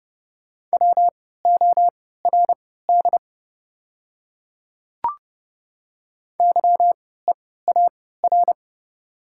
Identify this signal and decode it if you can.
Morse code, heard as a keyed tone, about 30 wpm, spaced as for 17 wpm, 700 Hz. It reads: WORD YEAR
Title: 3x Character Spacing - Top 200 Words - 30wpm